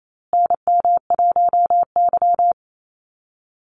Morse code, heard as a keyed tone, about 28 wpm, 700 Hz. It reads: NM1Y
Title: Call Signs Repeated 2x - 28wpm